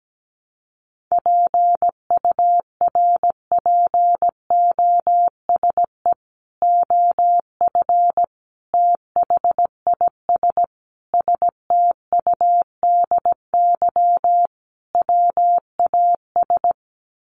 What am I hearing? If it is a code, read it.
Morse code, heard as a keyed tone, about 17 wpm, 700 Hz. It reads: PURPOSE OF THIS STUDY WAS
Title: Sets of 5 Words 17wpm